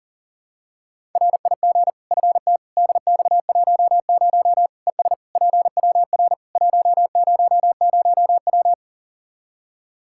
Morse code, heard as a keyed tone, about 37 wpm, 700 Hz. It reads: RIG FTDX10 ES PWR 100W
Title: Practice QSO - Exchange 2 - Info - 40wpm